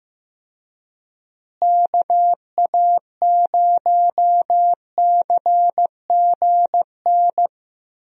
Morse code, heard as a keyed tone, about 16 wpm, 700 Hz. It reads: KA0CGN
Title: USA Call Signs - 6 Characters - Repeated 3x - 15wpm